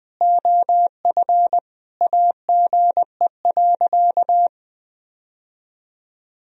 Morse code, heard as a keyed tone, about 20 wpm, 700 Hz. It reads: OF AGE.
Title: Sentences from Top 1000 Words - Review Part 3 - 20wpm